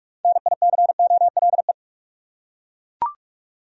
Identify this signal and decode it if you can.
Morse code, heard as a keyed tone, about 41 wpm, 700 Hz. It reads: NICOLE
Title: USA Common Names - 45wpm